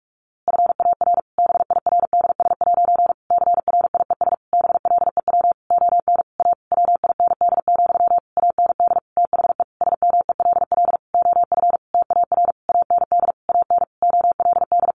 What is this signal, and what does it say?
Morse code, heard as a keyed tone, about 44 wpm, 700 Hz. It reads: FAR BIRDS' CRIES BLEW ON A WIND, AND THE SMELL OF TAR AND AN OLD
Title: A Sound Of Thunder 45wpm